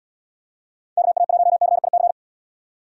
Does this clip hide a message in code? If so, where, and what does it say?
Yes: Morse code from 1.0 to 2.1 s, heard as a keyed tone, about 79 wpm, 700 Hz. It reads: KA1CIC